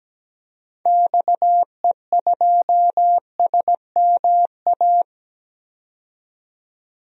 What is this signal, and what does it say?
Morse code, heard as a keyed tone, about 17 wpm, 700 Hz. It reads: XE2SMA
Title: DXCC-100 Call Signs - 6 Characters - 17wpm